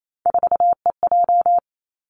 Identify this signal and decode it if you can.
Morse code, heard as a keyed tone, about 28 wpm, 700 Hz. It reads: 4EJ
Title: Three Challenging Letter-Number Combinations 28wpm